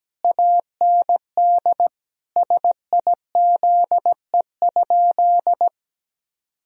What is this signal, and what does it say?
Morse code, heard as a keyed tone, about 17 wpm, 700 Hz. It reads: AND SIZE?
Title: Sentences from Top 1000 Words - Review Part 3 - 17wpm